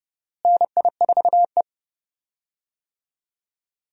Morse code, heard as a keyed tone, about 31 wpm, 700 Hz. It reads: NI4E
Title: USA Call Signs - 4 Characters - Repeated 2x - 30wpm